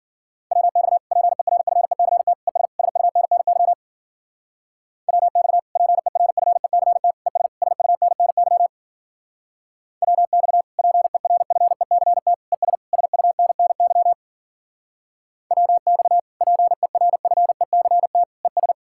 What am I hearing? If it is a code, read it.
Morse code, heard as a keyed tone, about 52 wpm, 700 Hz. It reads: WX PERFECT ES SUNNY WX PERFECT ES SUNNY WX PERFECT ES SUNNY WX PERFECT ES